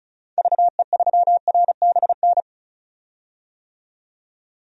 Morse code, heard as a keyed tone, about 36 wpm, 700 Hz. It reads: VE3RBN
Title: DXCC-100 Call Signs - 6 Characters - 35wpm